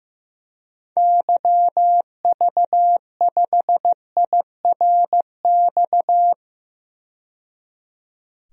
Morse code, heard as a keyed tone, about 15 wpm, 700 Hz. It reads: YV5IRX
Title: DXCC-100 Call Signs - 6 Characters - 15wpm